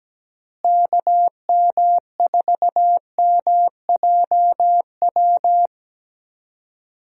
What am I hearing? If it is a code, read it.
Morse code, heard as a keyed tone, about 17 wpm, 700 Hz. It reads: KM4MJW
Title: USA Call Signs - 6 Characters - Repeated 3x - 17wpm